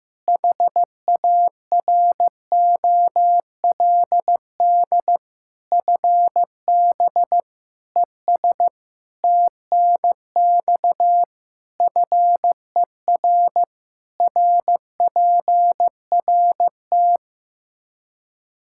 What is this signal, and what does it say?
Morse code, heard as a keyed tone, about 15 wpm, 700 Hz. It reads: HAROLD FB ES TNX FER RPRT
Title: Practice QSO - Exchange 2 - Info - 15wpm